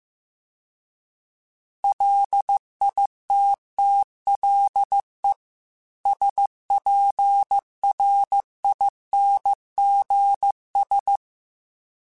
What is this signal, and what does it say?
Morse code, heard as a keyed tone, about 15 wpm, 800 Hz. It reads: LITTLE SPRINGS